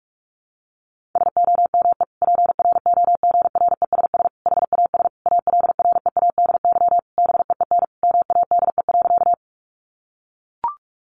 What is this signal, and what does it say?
Morse code, heard as a keyed tone, about 44 wpm, 700 Hz. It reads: SOME PROGRESS HAS ALREADY BEEN MADE.
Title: Sentences from Top 1400 Words 45wpm